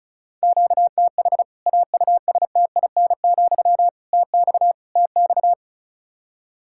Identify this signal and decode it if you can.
Morse code, heard as a keyed tone, about 36 wpm, 700 Hz. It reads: QTH AUSTIN, TX TX
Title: K1USN SST QSO Practice 35wpm